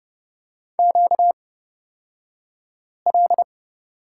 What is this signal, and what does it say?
Morse code, heard as a keyed tone, about 30 wpm, spaced as for 5 wpm, 700 Hz. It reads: QL
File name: Lesson 38d - Letter q - Two Characters 30wpm